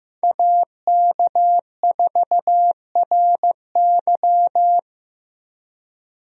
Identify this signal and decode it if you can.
Morse code, heard as a keyed tone, about 15 wpm, 700 Hz. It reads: AK4RY
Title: USA Call Signs - 5 Characters - Repeated 3x - 15wpm